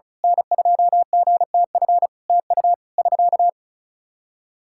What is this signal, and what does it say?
Morse code, heard as a keyed tone, about 36 wpm, 700 Hz. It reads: N2GTF TU <SK>
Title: K1USN SST QSO Practice 35wpm